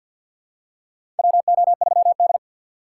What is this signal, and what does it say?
Morse code, heard as a keyed tone, about 47 wpm, 700 Hz. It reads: WO3D
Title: USA Call Signs - 4 Characters - 6x Character Spacing - 50wpm